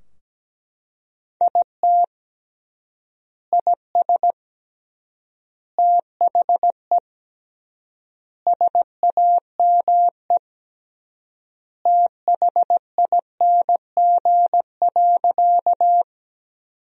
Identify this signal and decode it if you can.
Morse code, heard as a keyed tone, about 17 wpm, 700 Hz. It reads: IT IS THE SAME THING.
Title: 3x Word-Spacing - Sentences from Top 200 Words - 17wpm